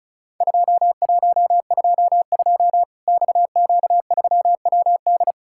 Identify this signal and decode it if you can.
Morse code, heard as a keyed tone, about 34 wpm, 700 Hz. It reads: 2122 XQ3WD
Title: Worked All Europe DX Contest - QTC Exchange - 35wpm